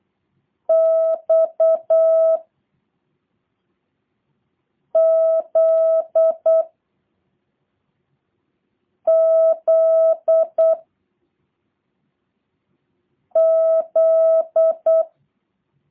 Morse code, heard as a keyed tone, about 8 wpm, 650 Hz. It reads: X Z Z Z